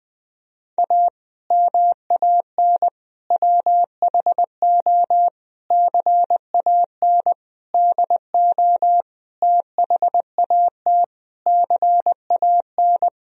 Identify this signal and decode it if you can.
Morse code, heard as a keyed tone, about 21 wpm, 700 Hz. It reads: A MAN WHO CAN DO THAT CAN
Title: Sentences from Top 1000 Words - Review Part 2 - 20wpm